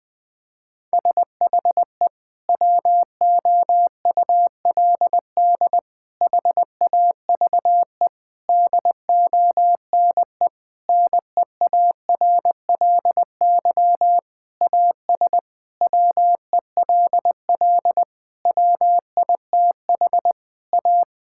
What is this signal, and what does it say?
Morse code, heard as a keyed tone, about 20 wpm, 700 Hz. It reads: SHE WOULD HAVE DONE NEARLY AS WELL WITH A
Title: Sentences from Top 1600 Words 20wpm